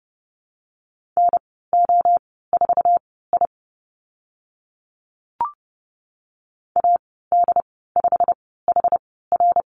Morse code, heard as a keyed tone, about 31 wpm, spaced as for 17 wpm, 700 Hz. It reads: NO4I AD5HR
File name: Lesson 15g - Number 5 - Call Signs 30wpm